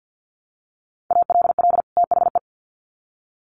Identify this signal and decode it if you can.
Morse code, heard as a keyed tone, about 50 wpm, 700 Hz. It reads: ALL THE
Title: Sets of 2 Words 50wpm